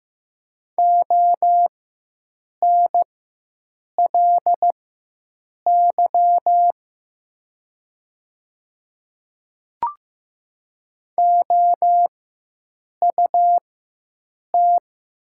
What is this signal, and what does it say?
Morse code, heard as a keyed tone, about 15 wpm, spaced as for 7 wpm, 700 Hz. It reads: ONLY OUT
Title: 4x Character Spacing - Top 200 Words - 15wpm